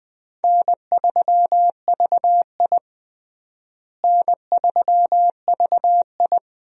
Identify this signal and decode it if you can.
Morse code, heard as a keyed tone, about 21 wpm, 700 Hz. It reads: N3VI N3VI
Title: USA Call Signs - 4 Characters - Repeated 2x - 20wpm